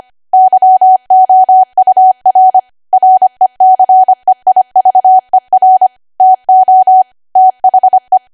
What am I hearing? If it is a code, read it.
Morse code, heard as a keyed tone, about 25 wpm, 750 Hz. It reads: YOUR RECEIVER TO THE
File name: Archive / 25 WPM / 25WPM